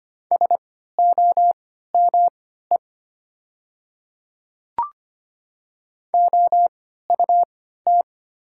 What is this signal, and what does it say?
Morse code, heard as a keyed tone, about 25 wpm, spaced as for 14 wpm, 700 Hz. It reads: SOME OUT
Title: 3x Character Spacing - Top 100 Words - 25wpm